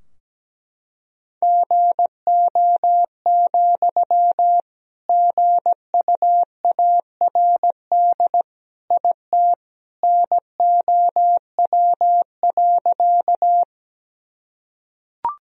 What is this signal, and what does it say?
Morse code, heard as a keyed tone, about 17 wpm, 700 Hz. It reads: GO, GUARD IT NOW.
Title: Sentences from Top 1000 Words - Review Part 4 - 17wpm